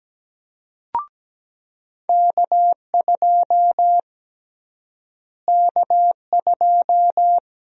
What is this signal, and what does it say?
Morse code, heard as a keyed tone, about 17 wpm, 700 Hz. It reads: K2 K2
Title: USA Call Sign Prefixes - Repeated 2x - 17wpm